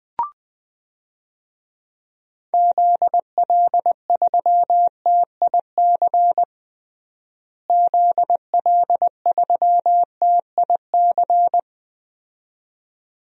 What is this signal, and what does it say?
Morse code, heard as a keyed tone, about 20 wpm, 700 Hz. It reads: ZL3TIC ZL3TIC
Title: DXCC-100 Call Signs - 6 Characters - Repeated 2x - 20wpm